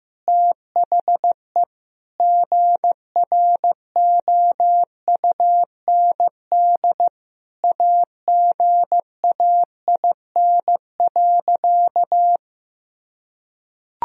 Morse code, heard as a keyed tone, about 15 wpm, 700 Hz. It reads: THE GROUND AGAIN.
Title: Sentences from Top 1000 Words - Review Part 2 - 15wpm